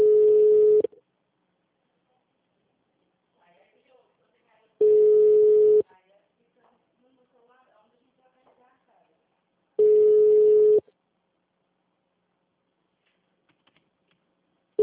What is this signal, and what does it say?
Ringback tone (Europe, 425 Hz), with speech.